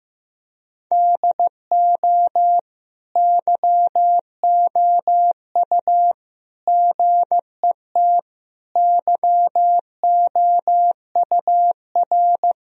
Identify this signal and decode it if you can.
Morse code, heard as a keyed tone, about 15 wpm, 700 Hz. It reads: DO YOU GET YOUR
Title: Sentences from Top 1000 Words - Review Part 4 - 15wpm